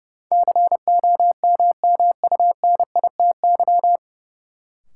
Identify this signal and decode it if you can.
Morse code, heard as a keyed tone, about 30 wpm, 700 Hz. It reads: COMMUNITY
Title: Sentences from Top 1000 Words - Review Part 4 - New Word Drill - 30wpm